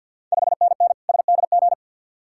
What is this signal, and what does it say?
Morse code, heard as a keyed tone, about 51 wpm, 700 Hz. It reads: 5NN SDG
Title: Ontario QSO Party Exchange 50wpm